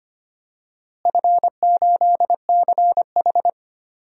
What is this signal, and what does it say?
Morse code, heard as a keyed tone, about 25 wpm, 700 Hz. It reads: F8CH
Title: DXCC-100 Call Signs - 4 Characters - Repeated 3x - 25wpm